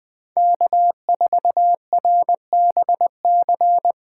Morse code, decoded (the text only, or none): K4RBC